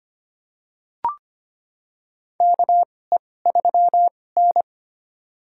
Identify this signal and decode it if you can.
Morse code, heard as a keyed tone, about 25 wpm, spaced as for 18 wpm, 700 Hz. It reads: KE3N
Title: USA Call Signs - 4 Characters - 2x Character Spacing - 25wpm